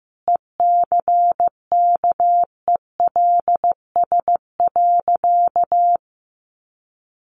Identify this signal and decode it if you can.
Morse code, heard as a keyed tone, about 15 wpm, 700 Hz. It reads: ECKELS.
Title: A Sound Of Thunder 15wpm